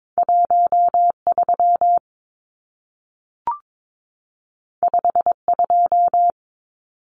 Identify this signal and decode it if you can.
Morse code, heard as a keyed tone, about 22 wpm, 700 Hz. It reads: 13 52
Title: Numbers - 2-Digit 22wpm